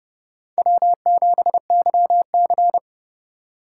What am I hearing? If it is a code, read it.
Morse code, heard as a keyed tone, about 32 wpm, 700 Hz. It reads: W7YC
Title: USA Call Signs - 4 Characters - 2x Character Spacing - 30wpm